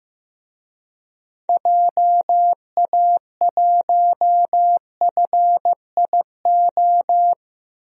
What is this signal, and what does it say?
Morse code, heard as a keyed tone, about 15 wpm, 700 Hz. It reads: JA1FIO
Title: DXCC-100 Call Signs - 6 Characters - 15wpm